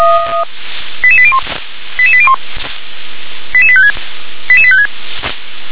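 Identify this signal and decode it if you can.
Five-tone selective call (ZVEI), heard as quick runs of five tones, tones of about 70 ms. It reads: 88801 88801 88856 88856
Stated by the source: ZVEI five-tone call 88824 received on SDR.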